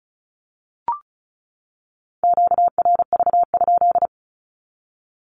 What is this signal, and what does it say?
Morse code, heard as a keyed tone, about 34 wpm, 700 Hz. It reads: QRV?
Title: Q-Codes 35wpm